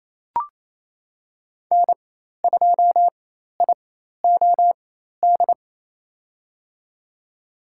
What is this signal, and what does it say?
Morse code, heard as a keyed tone, about 27 wpm, spaced as for 13 wpm, 700 Hz. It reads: N2IOD
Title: USA Call Signs - 5 Characters - 4x Character Spacing - 28wpm